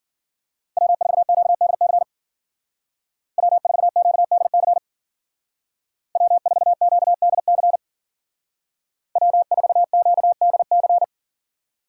Morse code, heard as a keyed tone, about 49 wpm, 700 Hz. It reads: W4QDC W4QDC W4QDC W4QDC
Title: Speed Racing - USA 1x3 Call Signs 40wpm